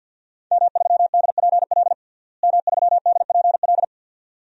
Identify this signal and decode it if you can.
Morse code, heard as a keyed tone, about 52 wpm, 700 Hz. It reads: M3DPL M3DPL
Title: DXCC-100 Call Signs - 5 Characters - Repeated 2x - 50wpm